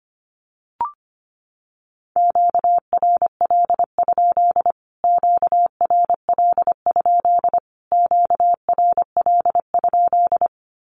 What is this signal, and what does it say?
Morse code, heard as a keyed tone, about 25 wpm, 700 Hz. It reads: QRL? QRL? QRL?
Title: Q-Codes Repeated 3x 25wpm